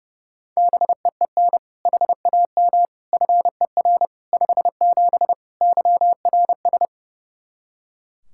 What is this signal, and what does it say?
Morse code, heard as a keyed tone, about 30 wpm, 700 Hz. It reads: BEEN HAM FER 57 YRS
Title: K1USN SST QSO Practice 30wpm